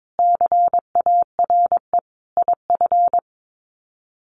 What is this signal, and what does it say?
Morse code, heard as a keyed tone, about 22 wpm, 700 Hz. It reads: CARE IF
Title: Sets of 5 Words 22wpm